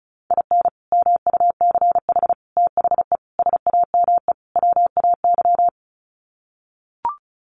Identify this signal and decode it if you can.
Morse code, heard as a keyed tone, about 34 wpm, 700 Hz. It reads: IN MUCH THE SAME WAY
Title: Sets of 5 Words Encore 35wpm